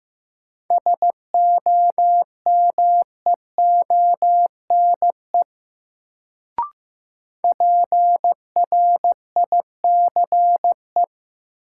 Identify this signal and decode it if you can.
Morse code, heard as a keyed tone, about 15 wpm, 700 Hz. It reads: SOMEONE PRICE
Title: Sentences from Top 600 Words - New Word Drill - 15wpm